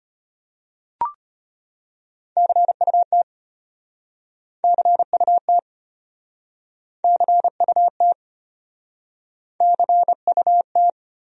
Morse code, heard as a keyed tone, about 36 wpm, 700 Hz. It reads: CUT CUT CUT CUT